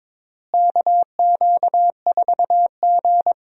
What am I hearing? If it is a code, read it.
Morse code, heard as a keyed tone, about 22 wpm, 700 Hz. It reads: KQ4G
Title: USA Call Signs - 4 Characters - Repeated 2x - 22wpm